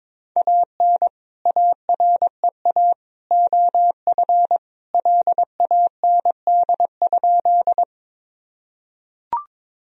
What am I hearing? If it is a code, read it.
Morse code, heard as a keyed tone, about 22 wpm, 700 Hz. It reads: AN AREA OF LAND?
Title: Sentences from Top 1000 Words - Review Part 2 - 22wpm